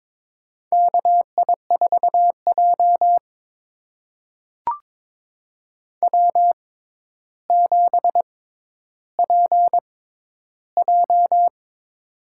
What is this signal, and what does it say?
Morse code, heard as a keyed tone, about 22 wpm, spaced as for 11 wpm, 700 Hz. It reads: KI4J W7PJ
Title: USA Call Signs - 4 Characters - 6x Character Spacing - 22wpm